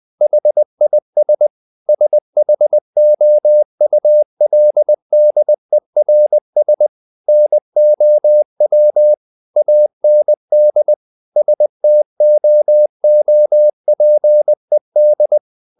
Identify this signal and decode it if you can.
Morse code, heard as a keyed tone, about 20 wpm, 600 Hz. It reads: HIS SHOULDERS NOW AND STOOPED
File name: the old man and the sea part 2 20wpm